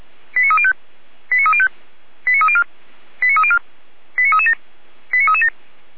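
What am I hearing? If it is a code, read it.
Five-tone selective call (ZVEI), heard as quick runs of five tones, tones of about 70 ms. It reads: 89305 89305 89304 89304 89337 89337
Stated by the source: ZVEI five-tone call 89246 received on an SDR.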